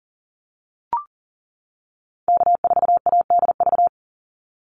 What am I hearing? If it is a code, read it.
Morse code, heard as a keyed tone, about 40 wpm, 700 Hz. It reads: K4ADV
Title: NA SOTA Activator Callsigns 40wpm